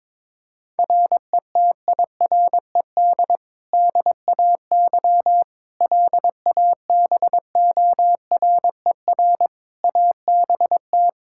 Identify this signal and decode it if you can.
Morse code, heard as a keyed tone, about 22 wpm, 700 Hz. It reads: RETIRED DAY LABORER ABT
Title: Practice QSO - Exchange 3 - More Info - 22wpm